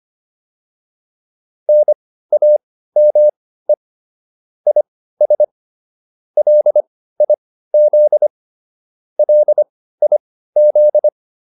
Morse code, heard as a keyed tone, about 25 wpm, spaced as for 15 wpm, 600 Hz. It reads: NAME IS LIZ LIZ
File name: QSO304 15